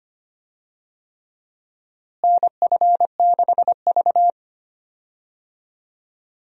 Morse code, heard as a keyed tone, about 25 wpm, 700 Hz. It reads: NF6V
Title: USA Call Signs - 4 Characters - Repeated 3x - 25wpm